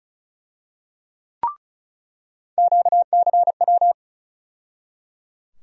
Morse code, heard as a keyed tone, about 34 wpm, 700 Hz. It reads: QCW
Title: USA Call Sign Postfixes - 3 Letters 35wpm